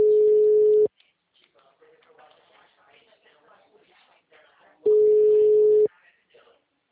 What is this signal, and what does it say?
Ringback tone (Europe, 425 Hz), with speech.